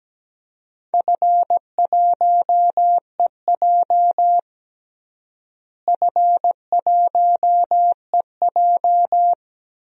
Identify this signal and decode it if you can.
Morse code, heard as a keyed tone, about 17 wpm, 700 Hz. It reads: F1EJ F1EJ
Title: DXCC-100 Call Signs - 4 Characters - Repeated 2x - 17wpm